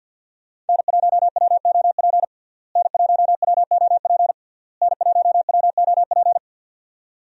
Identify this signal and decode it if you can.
Morse code, heard as a keyed tone, about 48 wpm, 700 Hz. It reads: N1WOP N1WOP N1WOP
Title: USA Call Signs - 5 Characters - Repeated 3x - 50wpm